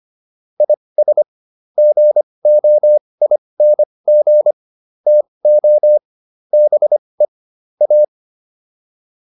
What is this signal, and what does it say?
Morse code, heard as a keyed tone, about 25 wpm, spaced as for 20 wpm, 600 Hz. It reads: IS GOING TO BE A